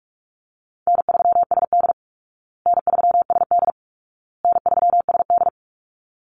Morse code, heard as a keyed tone, about 44 wpm, 700 Hz. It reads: N3SD N3SD N3SD
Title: Easy Call Signs Repeated 3x - 45wpm